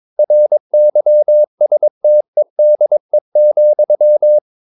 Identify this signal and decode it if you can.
Morse code, heard as a keyed tone, about 22 wpm, 600 Hz. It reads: RYSTEDE,
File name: Ferie QRS 22wpm